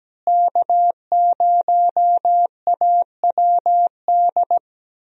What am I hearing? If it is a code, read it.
Morse code, heard as a keyed tone, about 17 wpm, 700 Hz. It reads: K0AWD
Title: USA Call Signs - 5 Characters - Repeated 2x - 17wpm